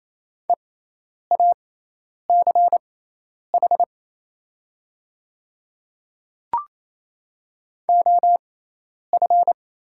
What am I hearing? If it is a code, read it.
Morse code, heard as a keyed tone, about 27 wpm, spaced as for 10 wpm, 700 Hz. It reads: EACH OF
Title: 6x Character Spacing - Top 100 Words - 28wpm